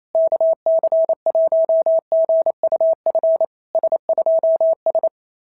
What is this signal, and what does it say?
Morse code, heard as a keyed tone, about 25 wpm, 650 Hz. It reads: KC1GUF S2S
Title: SOTA QSO Practice 28wpm